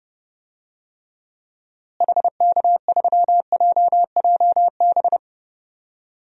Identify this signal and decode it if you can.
Morse code, heard as a keyed tone, about 30 wpm, 700 Hz. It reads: HK3JJB
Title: DXCC-100 Call Signs - 6 Characters - Repeated 2x - 30wpm